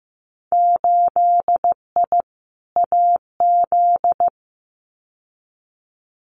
Morse code, heard as a keyed tone, about 15 wpm, 700 Hz. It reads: 8I AZ
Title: Winter Field Day Exchanges Encore 15wpm